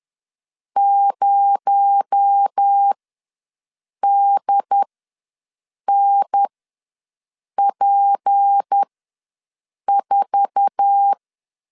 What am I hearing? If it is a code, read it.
Morse code, heard as a keyed tone, about 10 wpm, spaced as for 6 wpm, 800 Hz. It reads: 0DNP4
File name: LEZIONI / 8 YJ45 / 8 YJ4 ws 10wpm 4020